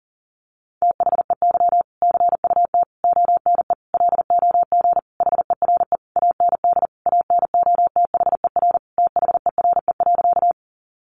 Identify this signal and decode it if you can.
Morse code, heard as a keyed tone, about 40 wpm, 700 Hz. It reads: THEY CUT ONE LOG HERE AND ANOTHER THERE.
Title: Sentences from Top 1400 Words 40wpm